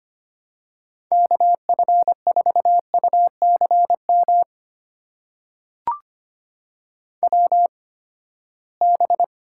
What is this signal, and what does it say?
Morse code, heard as a keyed tone, about 25 wpm, spaced as for 17 wpm, 700 Hz. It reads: KF4UCM WB
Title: USA Call Signs - 6 Characters - 8x Character Spacing - 25wpm